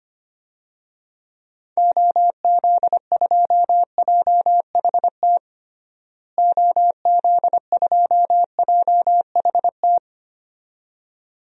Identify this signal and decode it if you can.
Morse code, heard as a keyed tone, about 25 wpm, 700 Hz. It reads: OZ2JHT OZ2JHT
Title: DXCC-100 Call Signs - 6 Characters - Repeated 2x - 25wpm